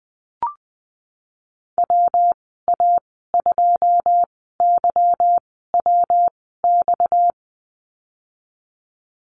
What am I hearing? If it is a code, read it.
Morse code, heard as a keyed tone, about 20 wpm, spaced as for 14 wpm, 700 Hz. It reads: WA2YWX
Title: 245 Lesson 39g - Letter z - Call Signs 20wpm